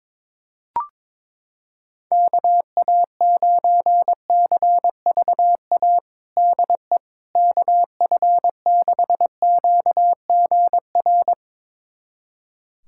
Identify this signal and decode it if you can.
Morse code, heard as a keyed tone, about 21 wpm, 700 Hz. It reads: KA9CVA DE KF6QGR
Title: USA Callsign DE Callsign 22wpm